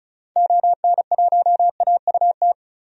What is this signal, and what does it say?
Morse code, heard as a keyed tone, about 34 wpm, 700 Hz. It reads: ON1AUT
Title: DXCC-100 Call Signs - 6 Characters - 35wpm